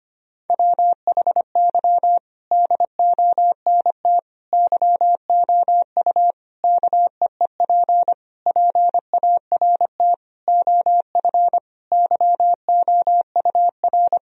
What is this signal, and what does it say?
Morse code, heard as a keyed tone, about 25 wpm, 700 Hz. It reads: WHY DONT YOU KEEP PART OF YOUR
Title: Sentences from Top 1000 Words - Review Part 4 - 25wpm